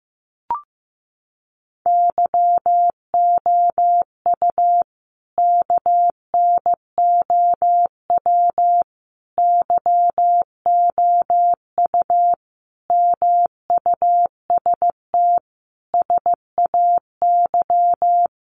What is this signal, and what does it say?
Morse code, heard as a keyed tone, about 15 wpm, 700 Hz. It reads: YOU KNOW YOU MUST SAY
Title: Sentences from Top 500 Words 15wpm